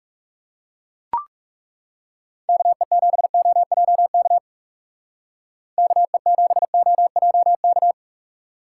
Morse code, heard as a keyed tone, about 35 wpm, 700 Hz. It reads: KE7OJK KE7OJK
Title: Speed Racing - USA 2x3 Call Signs 30wpm